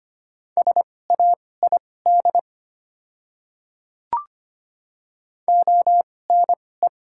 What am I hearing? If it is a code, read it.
Morse code, heard as a keyed tone, about 25 wpm, spaced as for 18 wpm, 690 Hz. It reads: SAID ONE
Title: 2x Character Spacing - Top 200 Words - 25wpm